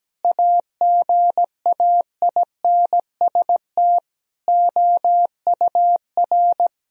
Morse code, heard as a keyed tone, about 17 wpm, 700 Hz. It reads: AGAINST OUR
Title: Sentences from Top 1000 Words - Review Part 4 - 17wpm